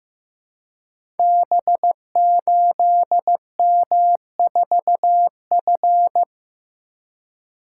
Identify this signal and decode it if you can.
Morse code, heard as a keyed tone, about 15 wpm, 700 Hz. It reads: B8M4F
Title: Code Groups - 5 Letter:Numbers - 15wpm